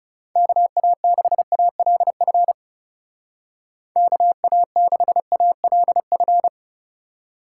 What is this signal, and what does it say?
Morse code, heard as a keyed tone, about 36 wpm, 700 Hz. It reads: KA6ALF KA6ALF
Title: Speed Racing - USA 2x3 Call Signs 30wpm